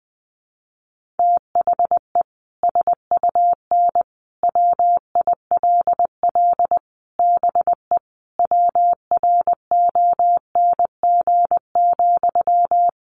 Morse code, heard as a keyed tone, about 20 wpm, 700 Hz. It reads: THE SUN WILL BE WRONG,
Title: Sentences from Top 900 Words 20wpm